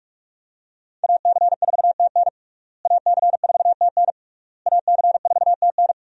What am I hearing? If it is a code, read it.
Morse code, heard as a keyed tone, about 47 wpm, 700 Hz. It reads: AC4TN AC4TN AC4TN
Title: USA Call Signs - 5 Characters - Repeated 3x - 45wpm